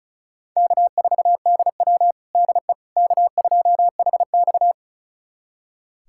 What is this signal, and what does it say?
Morse code, heard as a keyed tone, about 36 wpm, 700 Hz. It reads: K4DW DE K2HX
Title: USA Callsign DE Callsign - 4 Characters - 35wpm